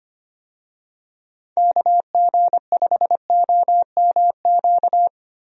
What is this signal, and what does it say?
Morse code, heard as a keyed tone, about 25 wpm, 700 Hz. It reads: KG5OMQ
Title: USA Call Signs - 6 Characters - Repeated 2x - 25wpm